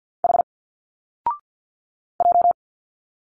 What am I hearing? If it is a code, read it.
Morse code, heard as a keyed tone, about 50 wpm, 700 Hz. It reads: H J
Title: Single Letters 50wpm